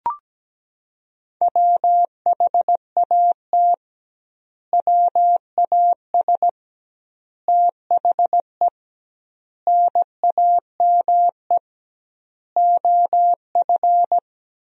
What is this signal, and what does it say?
Morse code, heard as a keyed tone, about 17 wpm, 700 Hz. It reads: WHAT WAS THE NAME OF
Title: 2x Word-Spacing - Sentences from Top 300 Words - 17wpm